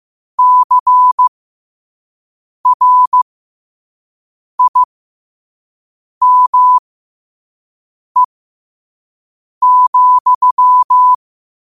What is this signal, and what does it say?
Morse code, heard as a keyed tone, about 15 wpm, spaced as for 5 wpm, 1000 Hz. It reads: CRIME,
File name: qotdfiles / 5wpm